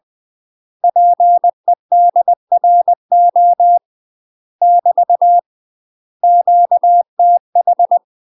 Morse code, heard as a keyed tone, about 20 wpm, 700 Hz. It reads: PEDRO <BT> QTH
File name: UK MASTER QSOs TRAINING / 20WPM / UK MASTER QSOs 0066